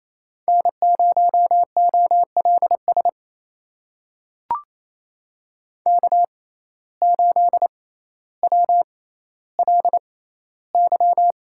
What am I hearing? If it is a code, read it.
Morse code, heard as a keyed tone, about 28 wpm, spaced as for 14 wpm, 700 Hz. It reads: N0OLS K8WLY
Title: USA Call Signs - 5 Characters - 6x Character Spacing - 28wpm